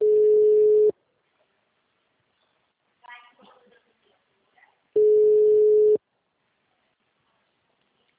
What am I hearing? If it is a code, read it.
Ringback tone (Europe, 425 Hz), with speech.